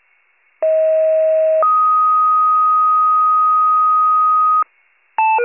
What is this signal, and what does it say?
Two-tone page, heard as a one-second tone, then a three-second tone, about 640 then 1250 Hz. It is heard from 0.6 to 4.6 s.